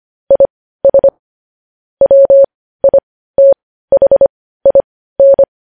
Morse code, heard as a keyed tone, about 25 wpm, spaced as for 15 wpm, 550 Hz. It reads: IS WITHIN
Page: news-headlines-in-morse-code-at-15-wpm